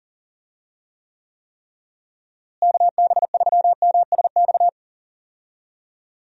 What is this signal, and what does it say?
Morse code, heard as a keyed tone, about 37 wpm, 700 Hz. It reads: KB3MSX
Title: USA Call Signs - 6 Characters - Repeated 3x - 40wpm